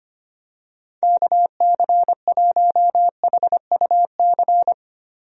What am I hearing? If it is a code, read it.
Morse code, heard as a keyed tone, about 25 wpm, 700 Hz. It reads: KC1HUC
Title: USA Call Signs - 6 Characters - Repeated 2x - 25wpm